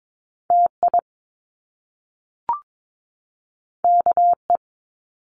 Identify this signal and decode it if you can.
Morse code, heard as a keyed tone, about 22 wpm, 700 Hz. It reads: TI KE
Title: Top Two Letter Combinations in Words 22wpm